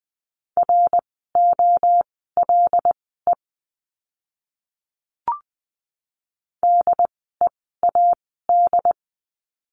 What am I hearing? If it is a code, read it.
Morse code, heard as a keyed tone, about 20 wpm, spaced as for 15 wpm, 700 Hz. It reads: ROLE DEAD
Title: 075 Lesson 13f - Letter L - Words 20wpm